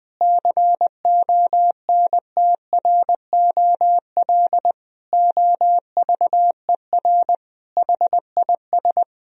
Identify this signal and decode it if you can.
Morse code, heard as a keyed tone, about 20 wpm, 700 Hz. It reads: CONTROL OVER HIS
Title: Sentences from Top 1000 Words - Review Part 3 - 20wpm